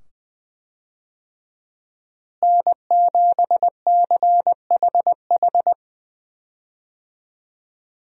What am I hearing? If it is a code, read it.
Morse code, heard as a keyed tone, about 20 wpm, 700 Hz. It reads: N7CHH
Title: USA Call Signs - 5 Characters - Repeated 2x - 20wpm